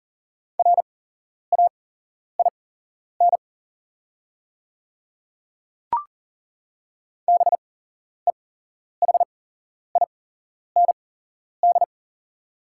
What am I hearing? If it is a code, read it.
Morse code, heard as a keyed tone, about 40 wpm, spaced as for 11 wpm, 690 Hz. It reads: RAIN BEHIND
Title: 8x Character Spacing - Top 500 Words - 40wpm